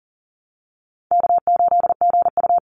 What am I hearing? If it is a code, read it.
Morse code, heard as a keyed tone, about 40 wpm, 700 Hz. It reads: K8GU
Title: Easy Call Signs Repeated 3x - 40wpm